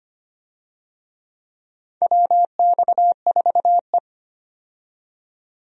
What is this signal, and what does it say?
Morse code, heard as a keyed tone, about 25 wpm, 700 Hz. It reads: WX4E
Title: USA Call Signs - 4 Characters - 25wpm